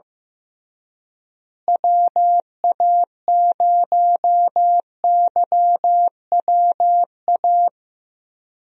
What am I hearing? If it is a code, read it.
Morse code, heard as a keyed tone, about 15 wpm, 700 Hz. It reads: WA0YWA
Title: USA Call Signs - 6 Characters - Repeated 2x - 15wpm